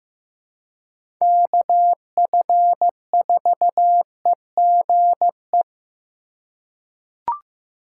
Morse code, heard as a keyed tone, about 15 wpm, 700 Hz. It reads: KF4EGE
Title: USA Call Signs - 6 Characters - 3x Character Spacing - 15wpm